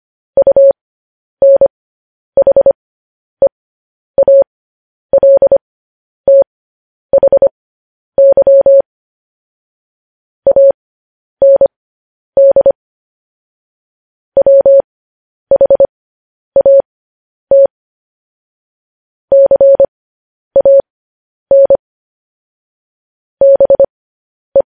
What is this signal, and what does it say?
Morse code, heard as a keyed tone, about 25 wpm, spaced as for 10 wpm, 550 Hz. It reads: UNHEALTHY AND WHAT CAN BE
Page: news headlines in morse code at 10 wpm